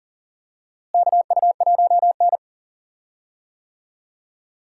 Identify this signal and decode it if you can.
Morse code, heard as a keyed tone, about 40 wpm, 700 Hz. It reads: KU1N
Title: USA Call Signs - 4 Characters - Repeated 2x - 40wpm